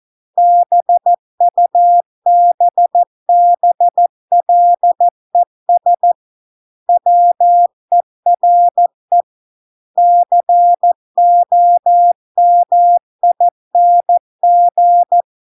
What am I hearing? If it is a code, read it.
Morse code, heard as a keyed tone, about 14 wpm, 700 Hz. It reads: BUBBLES WERE COMING